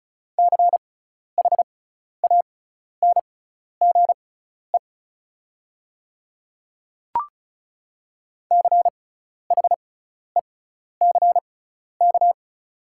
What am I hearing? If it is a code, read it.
Morse code, heard as a keyed tone, about 36 wpm, spaced as for 12 wpm, 700 Hz. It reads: CHANGE CHECK
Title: 6x Character Spacing - Top 500 Words - 35wpm